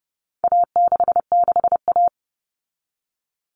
Morse code, heard as a keyed tone, about 30 wpm, 700 Hz. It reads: A66A
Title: All Cut Numbers - 4-Digit 30wpm